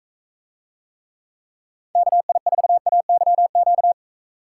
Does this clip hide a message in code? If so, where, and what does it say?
Yes: Morse code from 2.0 to 3.9 s, heard as a keyed tone, about 42 wpm, 700 Hz. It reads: KI4AYQ